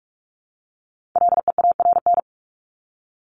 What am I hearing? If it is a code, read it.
Morse code, heard as a keyed tone, about 44 wpm, 700 Hz. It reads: LEARN
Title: ING Suffix - Top 100 Words with Base Words 45wpm